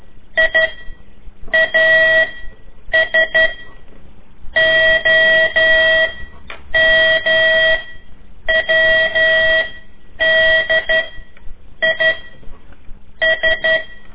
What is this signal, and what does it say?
Morse code, heard as a keyed tone, about 12 wpm, 1950 Hz. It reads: I A S O M W D I S
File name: ik1weg2 / CORSI CW / lez4-3